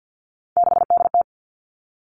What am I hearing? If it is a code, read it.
Morse code, heard as a keyed tone, about 50 wpm, 700 Hz. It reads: 6DT
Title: All Cut Numbers - 3-Digit 50wpm